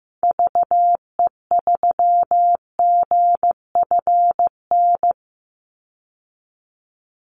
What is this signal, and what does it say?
Morse code, heard as a keyed tone, about 15 wpm, 700 Hz. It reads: VE3GFN
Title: Call Signs Repeated 2x - 15wpm